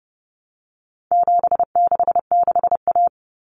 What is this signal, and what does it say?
Morse code, heard as a keyed tone, about 30 wpm, 700 Hz. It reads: 766A
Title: Common Cut Numbers - 4-Digit 30wpm